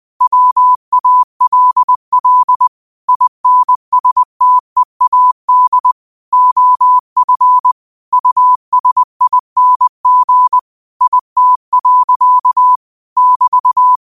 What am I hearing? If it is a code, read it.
Morse code, heard as a keyed tone, about 20 wpm, 1000 Hz. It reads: WALL INSTEAD OF USING IT. <BT>
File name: qotdfiles / 20wpm